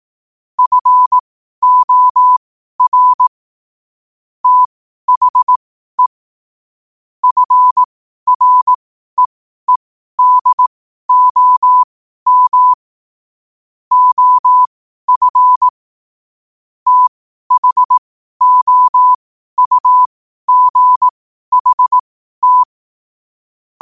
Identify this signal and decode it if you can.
Morse code, heard as a keyed tone, about 18 wpm, spaced as for 12 wpm, 1000 Hz. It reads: FOR THE FREEDOM OF THOUGHT
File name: qotdfiles / 12wpm